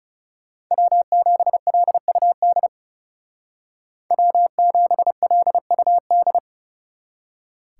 Morse code, heard as a keyed tone, about 39 wpm, 700 Hz. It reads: W7LUD W7LUD